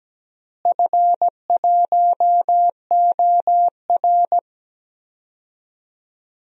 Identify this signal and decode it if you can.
Morse code, heard as a keyed tone, about 17 wpm, 700 Hz. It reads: F1OR
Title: DXCC-100 Call Signs - 4 Characters - Repeated 2x - 17wpm